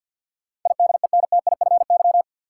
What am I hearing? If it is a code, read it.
Morse code, heard as a keyed tone, about 52 wpm, 700 Hz. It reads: IDENTIFY
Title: Sentences from Top 900 Words - New Word Drill - 50wpm